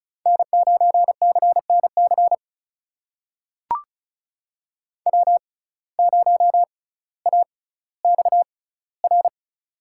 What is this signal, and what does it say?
Morse code, heard as a keyed tone, about 36 wpm, spaced as for 18 wpm, 700 Hz. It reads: N9CNC W0AXR
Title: USA Call Signs - 5 Characters - 6x Character Spacing - 35wpm